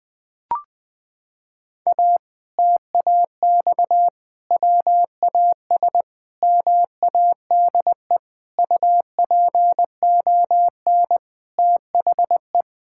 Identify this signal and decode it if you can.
Morse code, heard as a keyed tone, about 20 wpm, 700 Hz. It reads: A TAX WAS MADE UPON THE
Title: Sentences from Top 1000 Words - Review Part 6 - 20wpm